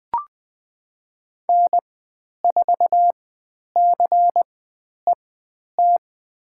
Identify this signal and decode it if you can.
Morse code, heard as a keyed tone, about 20 wpm, spaced as for 10 wpm, 700 Hz. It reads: N4CET
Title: 102 Lesson 17g - Letter u - Call Signs 20wpm